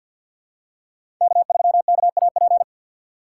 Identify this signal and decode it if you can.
Morse code, heard as a keyed tone, about 52 wpm, 700 Hz. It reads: K3KAP
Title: USA Call Signs - 5 Characters - 3x Character Spacing - 50wpm